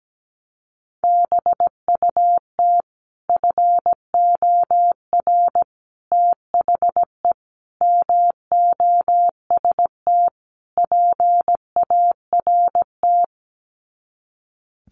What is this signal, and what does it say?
Morse code, heard as a keyed tone, about 17 wpm, 690 Hz. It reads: BUT FOR THE MOST PART
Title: Sets of 5 Words Encore 17wpm